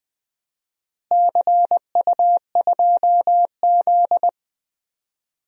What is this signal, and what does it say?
Morse code, heard as a keyed tone, about 20 wpm, 700 Hz. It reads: CU2Z
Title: DXCC-100 Call Signs - 4 Characters - Repeated 2x - 20wpm